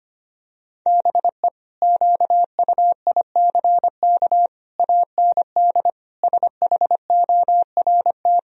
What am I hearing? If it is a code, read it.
Morse code, heard as a keyed tone, about 25 wpm, 700 Hz. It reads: BE QUICK AND SHORT
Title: Sentences from Top 1000 Words - Review Part 2 - 25wpm